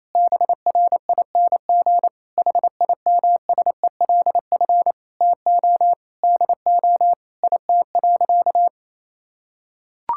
Morse code, heard as a keyed tone, about 28 wpm, 700 Hz. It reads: BRING HIMSELF TO DO IT.
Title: Sentences from Top 1600 Words 28wpm